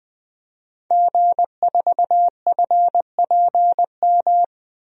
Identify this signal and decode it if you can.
Morse code, heard as a keyed tone, about 21 wpm, 700 Hz. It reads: G4FPM
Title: DXCC-100 Call Signs - 5 Characters - Repeated 2x - 20wpm